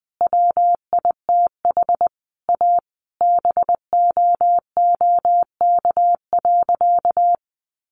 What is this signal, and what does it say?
Morse code, heard as a keyed tone, about 20 wpm, 700 Hz. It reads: WITH A BOOK.
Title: Easy Phrases 20wpm